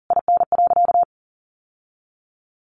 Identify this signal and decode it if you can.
Morse code, heard as a keyed tone, about 40 wpm, 700 Hz. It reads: IN.
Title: Sentences from Top 1500 Words 40wpm